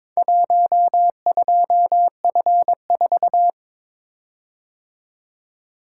Morse code, heard as a keyed tone, about 22 wpm, 700 Hz. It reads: 12F4
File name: practice / 22wpm / Four Letter-Number Combinations - 22wpm